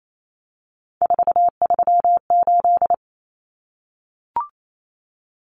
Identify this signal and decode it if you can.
Morse code, heard as a keyed tone, about 28 wpm, 700 Hz. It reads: 438
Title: Common Cut Numbers - 3-Digit 28wpm